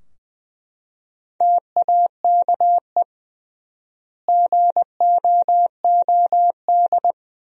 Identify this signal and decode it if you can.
Morse code, heard as a keyed tone, about 20 wpm, 700 Hz. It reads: TAKE GOOD
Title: 3x Word-Spacing - Sentences from Top 400 Words - 20wpm